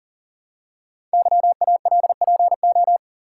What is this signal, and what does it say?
Morse code, heard as a keyed tone, about 43 wpm, 700 Hz. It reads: YALPO
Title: Code Groups - 5 Letters - 40wpm